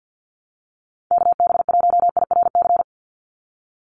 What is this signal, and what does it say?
Morse code, heard as a keyed tone, about 50 wpm, 700 Hz. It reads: KB1IRC
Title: USA Call Signs 50wpm